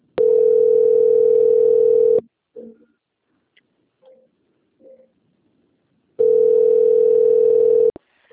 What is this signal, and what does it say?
Ringback tone (North America), 440 + 480 Hz.